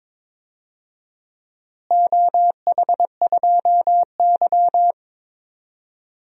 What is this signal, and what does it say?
Morse code, heard as a keyed tone, about 22 wpm, 700 Hz. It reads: OH2Y
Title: DXCC-100 Call Signs - 4 Characters - Repeated 2x - 22wpm